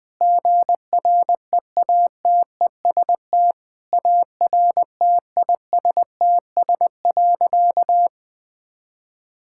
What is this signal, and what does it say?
Morse code, heard as a keyed tone, about 20 wpm, 700 Hz. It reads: GREATEST ARTISTS.
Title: Sentences with 8-Word Phrases - 20wpm